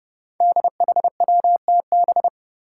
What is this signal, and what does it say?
Morse code, heard as a keyed tone, about 28 wpm, 700 Hz. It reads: DHWTB
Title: Five Challenging Letter-Number Combinations 30wpm